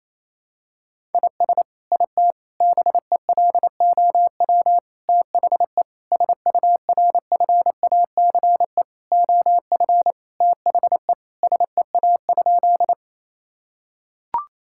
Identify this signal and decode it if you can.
Morse code, heard as a keyed tone, about 28 wpm, 700 Hz. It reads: IS IT BELOW THE SURFACE OF THE SEA?
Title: Sentences from Top 1000 Words - Review Part 5 - 28wpm